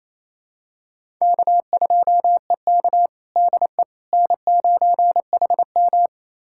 Morse code, heard as a keyed tone, about 28 wpm, 700 Hz. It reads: K2EK DE N9HM
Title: USA Callsign DE Callsign - 4 Characters - 28wpm